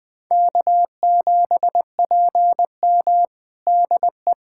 Morse code, heard as a keyed tone, about 20 wpm, 700 Hz. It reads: K7PM DE
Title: USA Callsign DE Callsign - 4 Characters - 20wpm